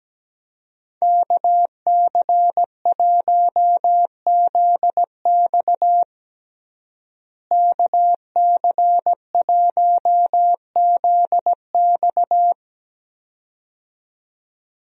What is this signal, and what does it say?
Morse code, heard as a keyed tone, about 17 wpm, 700 Hz. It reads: KC1ZX KC1ZX
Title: USA Call Signs - 5 Characters - Repeated 2x - 17wpm